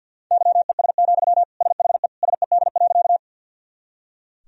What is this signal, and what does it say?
Morse code, heard as a keyed tone, about 46 wpm, 700 Hz. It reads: YES, SHE SED.
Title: Sentences from Top 500 Words and Abbreviations 50wpm